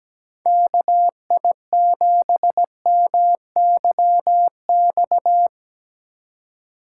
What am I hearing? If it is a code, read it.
Morse code, heard as a keyed tone, about 17 wpm, 700 Hz. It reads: KI7MYX
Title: USA Call Signs - 6 Characters - 2x Character Spacing - 17wpm